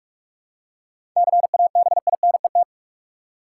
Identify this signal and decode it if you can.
Morse code, heard as a keyed tone, about 46 wpm, 700 Hz. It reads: CABINET
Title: Seven Letter Words - 45wpm